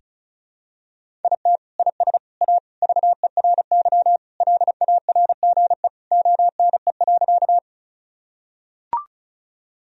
Morse code, heard as a keyed tone, about 36 wpm, 700 Hz. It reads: IT IS A VERY LARGE ONE.
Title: Sentences from Top 1000 Words - Review Part 1 - 35wpm